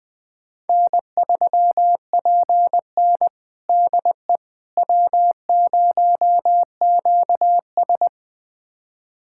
Morse code, heard as a keyed tone, about 21 wpm, 700 Hz. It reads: N3PN DE W0QS
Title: USA Callsign DE Callsign - 4 Characters - 20wpm